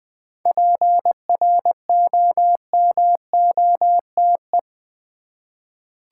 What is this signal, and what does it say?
Morse code, heard as a keyed tone, about 20 wpm, 700 Hz. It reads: PROMOTE
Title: 7 Letter Words - First 3 Letters Given - 20wpm